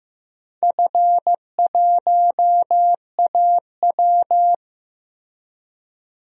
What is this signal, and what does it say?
Morse code, heard as a keyed tone, about 15 wpm, 700 Hz. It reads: F1AW
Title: DXCC-100 Call Signs - 4 Characters - Repeated 2x - 15wpm